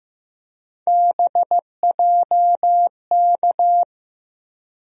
Morse code, heard as a keyed tone, about 15 wpm, 700 Hz. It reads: BJK
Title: USA Call Sign Postfixes - 3 Letters 15wpm